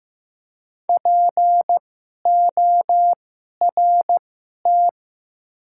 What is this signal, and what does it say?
Morse code, heard as a keyed tone, about 15 wpm, spaced as for 11 wpm, 700 Hz. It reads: PORT
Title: 2x Character Spacing - Top 500 Words - 15wpm